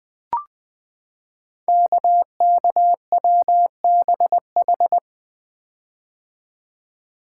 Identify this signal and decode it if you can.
Morse code, heard as a keyed tone, about 20 wpm, 700 Hz. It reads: KKWBH
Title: Code Groups - 5 Letters - 20wpm